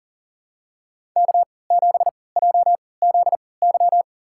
Morse code, heard as a keyed tone, about 40 wpm, spaced as for 23 wpm, 700 Hz. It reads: K7JZY
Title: USA Call Signs - 3x Character Spacing - 40wpm